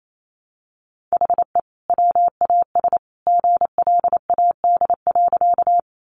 Morse code, heard as a keyed tone, about 28 wpm, 700 Hz. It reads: HE WAS GLAD.
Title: Sentences from Top 800 Words 28wpm